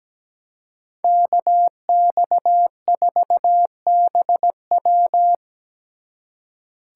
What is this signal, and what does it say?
Morse code, heard as a keyed tone, about 20 wpm, 700 Hz. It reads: KX4BW
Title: Speed Racing - USA Call Signs 17wpm